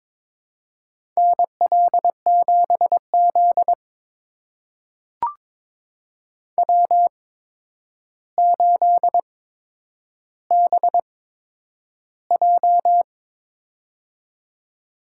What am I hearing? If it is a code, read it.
Morse code, heard as a keyed tone, about 22 wpm, spaced as for 9 wpm, 700 Hz. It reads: NL7Z W8BJ
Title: USA Call Signs - 4 Characters - 8x Character Spacing - 22wpm